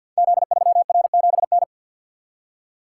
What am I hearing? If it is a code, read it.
Morse code, heard as a keyed tone, about 50 wpm, 700 Hz. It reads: Z3R7N
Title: Code Groups - 5 Letter:Numbers - 50wpm